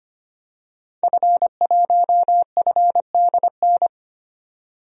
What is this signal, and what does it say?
Morse code, heard as a keyed tone, about 25 wpm, 700 Hz. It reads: F1FDN
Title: DXCC-100 Call Signs - 5 Characters - 25wpm